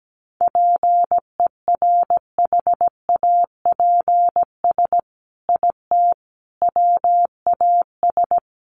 Morse code, heard as a keyed tone, about 17 wpm, 700 Hz. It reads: PERHAPS IT WAS
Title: Sentences from Top 1400 Words 17wpm